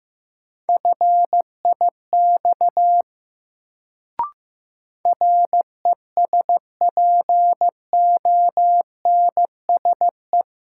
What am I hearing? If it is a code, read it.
Morse code, heard as a keyed tone, about 15 wpm, 700 Hz. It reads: FIX RESPONSE
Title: Sentences from Top 1400 Words - New Word Drill - 15wpm